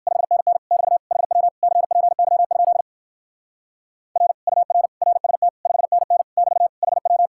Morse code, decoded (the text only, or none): 5NN <BT> HW CPY? R UR RST 5NN <BT> HW